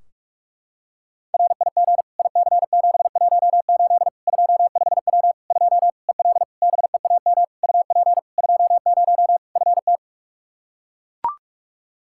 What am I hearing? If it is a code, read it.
Morse code, heard as a keyed tone, about 41 wpm, 700 Hz. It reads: RIG IC718 25W 2 EL BEAM UP 20 FT
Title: Practice QSO - Exchange 2 - Info - 45wpm